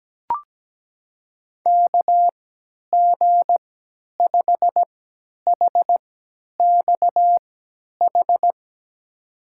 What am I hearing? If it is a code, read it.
Morse code, heard as a keyed tone, about 17 wpm, spaced as for 10 wpm, 700 Hz. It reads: KG5HXH
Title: USA Call Signs - 6 Characters - 3x Character Spacing - 17wpm